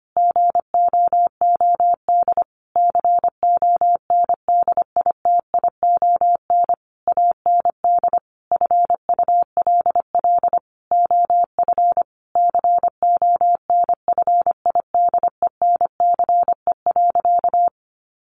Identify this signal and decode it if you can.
Morse code, heard as a keyed tone, about 25 wpm, 700 Hz. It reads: GOOD CONDITION AND FULL OF CONFIDENCE.
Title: Sentences from Top 1500 Words 25wpm